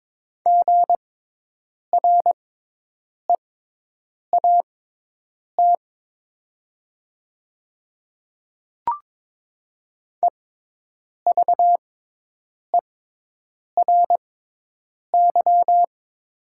Morse code, heard as a keyed tone, about 22 wpm, spaced as for 8 wpm, 700 Hz. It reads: GREAT EVERY
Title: 6x Character Spacing - Top 200 Words - 22wpm